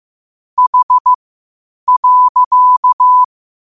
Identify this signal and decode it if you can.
Morse code, heard as a keyed tone, about 15 wpm, spaced as for 8 wpm, 1000 Hz. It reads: H.
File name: qotdfiles / 8wpm